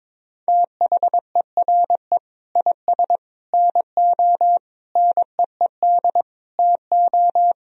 Morse code, heard as a keyed tone, about 22 wpm, 700 Hz. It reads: THERE IS NO NEED TO
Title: Sentences from Top 1000 Words - Review Part 2 - 22wpm